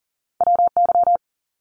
Morse code, heard as a keyed tone, about 39 wpm, 700 Hz. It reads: WY
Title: US States 40wpm